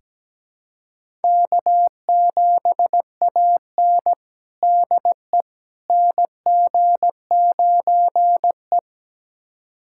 Morse code, heard as a keyed tone, about 17 wpm, 700 Hz. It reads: K7AN DE NG9E
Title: USA Callsign DE Callsign - 4 Characters - 17wpm